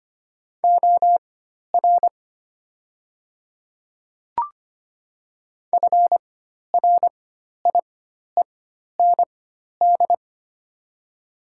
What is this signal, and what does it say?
Morse code, heard as a keyed tone, about 25 wpm, spaced as for 12 wpm, 700 Hz. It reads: OR FRIEND
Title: 4x Character Spacing - Top 300 Words - 25wpm